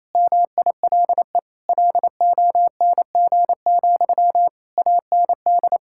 Morse code, heard as a keyed tone, about 27 wpm, 700 Hz. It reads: MILE LONG, AND
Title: Sentences from Top 1600 Words 28wpm